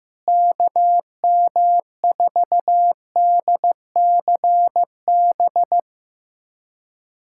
Morse code, heard as a keyed tone, about 15 wpm, 700 Hz. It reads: KM4DCB
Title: USA Callsign DE Callsign 15wpm